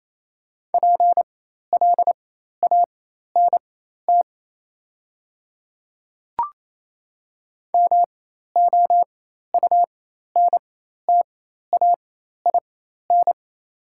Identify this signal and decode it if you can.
Morse code, heard as a keyed tone, about 28 wpm, spaced as for 13 wpm, 700 Hz. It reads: PLANT MOUNTAIN
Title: 4x Character Spacing - Top 500 Words - 28wpm